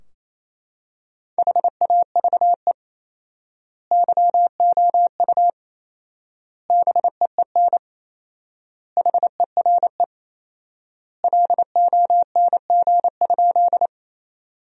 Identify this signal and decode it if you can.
Morse code, heard as a keyed tone, about 28 wpm, 700 Hz. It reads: HAVE YOU BEEN HERE LONG?
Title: 4x Word-Spacing - Sentences from Top 200 Words - 28wpm